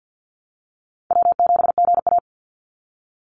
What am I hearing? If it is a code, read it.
Morse code, heard as a keyed tone, about 50 wpm, 700 Hz. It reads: W7GA
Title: NA SOTA Chaser Callsigns 50wpm